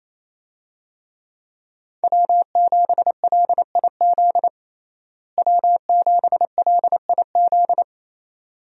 Morse code, heard as a keyed tone, about 28 wpm, 700 Hz. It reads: W7LIZ W7LIZ
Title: USA Call Signs - 5 Characters - Repeated 2x - 28wpm